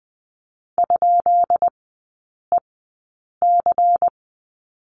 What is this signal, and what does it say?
Morse code, heard as a keyed tone, about 20 wpm, spaced as for 8 wpm, 700 Hz. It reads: ?EC
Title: 166 Lesson 27e - Number 7 - Three Characters 20wpm